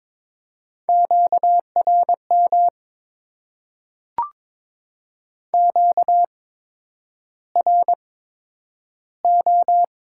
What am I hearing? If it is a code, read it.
Morse code, heard as a keyed tone, about 22 wpm, spaced as for 9 wpm, 690 Hz. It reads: QRM QRO
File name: Top 90 QSO Elements - 8x Character Spacing - 22wpm